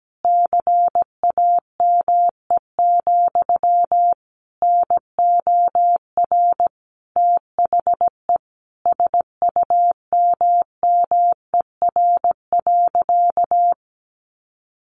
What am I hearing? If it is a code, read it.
Morse code, heard as a keyed tone, about 17 wpm, 700 Hz. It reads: CAME, NOR THE SUMMER.
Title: Sentences from Top 1500 Words 17wpm